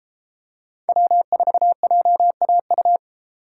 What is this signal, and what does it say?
Morse code, heard as a keyed tone, about 31 wpm, 700 Hz. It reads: W4JAU